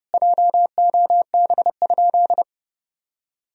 Morse code, heard as a keyed tone, about 30 wpm, 700 Hz. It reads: JOB?
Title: Sentences from Top 1000 Words - Review Part 3 - 30wpm